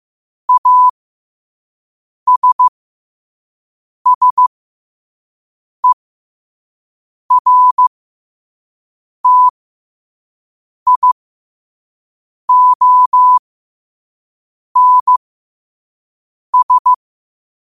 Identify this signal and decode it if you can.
Morse code, heard as a keyed tone, about 15 wpm, spaced as for 5 wpm, 1000 Hz. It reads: ASSERTIONS
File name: qotdfiles / 5wpm